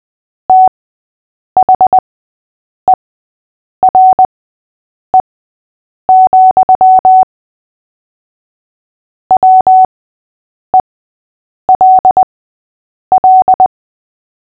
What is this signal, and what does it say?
Morse code, heard as a keyed tone, about 20 wpm, spaced as for 8 wpm, 750 Hz. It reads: THERE, WELL
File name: cw / fists / Short-Story-8-8wpm